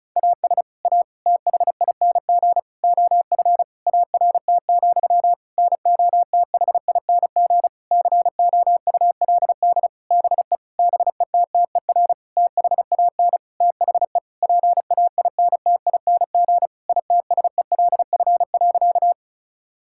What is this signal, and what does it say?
Morse code, heard as a keyed tone, about 36 wpm, 700 Hz. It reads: AS A THING OF ART, NOTHING COULD BE BETTER THAN THE PAINTING ITSELF.
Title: Sentences from Top 1000 Words - Review Part 6 - 35wpm